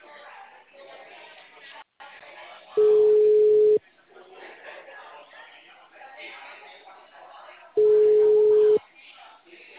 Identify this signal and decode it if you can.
Ringback tone (Europe, 425 Hz), with speech.
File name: out- -2015- - - .266